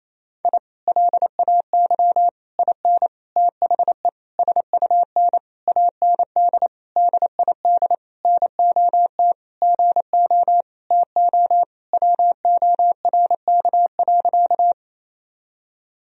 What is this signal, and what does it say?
Morse code, heard as a keyed tone, about 28 wpm, 700 Hz. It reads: I LAY IN THE SUN AND DID NOT GO TO WORK.
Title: Sentences from Top 1000 Words - Review Part 2 - 28wpm